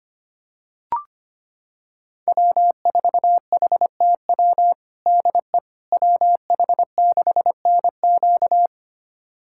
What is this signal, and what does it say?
Morse code, heard as a keyed tone, about 25 wpm, 700 Hz. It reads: W4HTW DE WH6NQ
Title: USA Callsign DE Callsign - 5 Characters - 25wpm